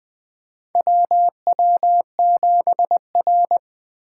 Morse code, heard as a keyed tone, about 20 wpm, 700 Hz. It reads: WW7R
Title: USA Call Signs - 4 Characters - 2x Character Spacing - 20wpm